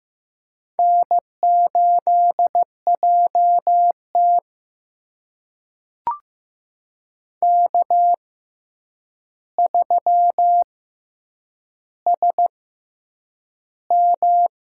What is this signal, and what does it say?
Morse code, heard as a keyed tone, about 15 wpm, spaced as for 8 wpm, 700 Hz. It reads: N8JT K3SM
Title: USA Call Signs - 4 Characters - 6x Character Spacing - 15wpm